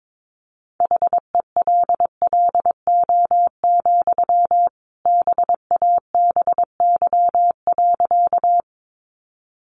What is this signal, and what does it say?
Morse code, heard as a keyed tone, about 22 wpm, 700 Hz. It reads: HELLO, BABY.
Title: Sentences from Top 1300 Words 22wpm